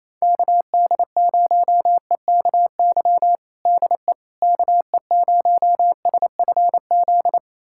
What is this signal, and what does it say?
Morse code, heard as a keyed tone, about 28 wpm, 700 Hz. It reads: KD0EKY DE KE0SFZ
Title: USA Callsign DE Callsign - 6 Characters - 28wpm